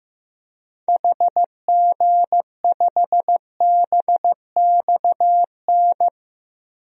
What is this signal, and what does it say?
Morse code, heard as a keyed tone, about 15 wpm, 700 Hz. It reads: HG5BXN
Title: DXCC-100 Call Signs - 6 Characters - 15wpm